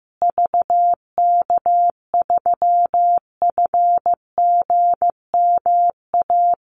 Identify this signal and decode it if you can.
Morse code, heard as a keyed tone, about 15 wpm, 700 Hz. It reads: VK3FGMA
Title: USA Call Signs Repeated 2x 15wpm